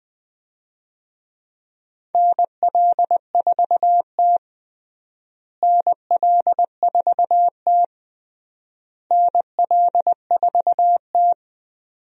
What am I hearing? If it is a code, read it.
Morse code, heard as a keyed tone, about 20 wpm, 700 Hz. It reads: NL4T NL4T NL4T
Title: USA Call Signs - 4 Characters - Repeated 3x - 20wpm